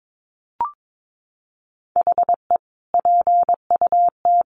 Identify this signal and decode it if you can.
Morse code, heard as a keyed tone, about 22 wpm, 700 Hz. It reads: HE PUT
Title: Sentences from Top 1400 Words 22wpm